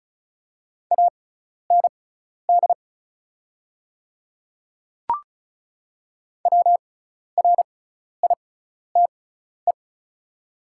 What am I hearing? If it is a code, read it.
Morse code, heard as a keyed tone, about 34 wpm, spaced as for 12 wpm, 700 Hz. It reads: AND WRITE
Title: 6x Character Spacing - Top 200 Words - 35wpm